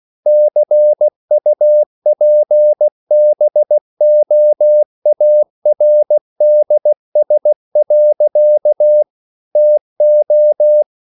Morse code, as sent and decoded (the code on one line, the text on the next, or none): -.-. ..- .--. -... --- .- .-. -.. ... .-.-.- / - ---
CUPBOARDS. TO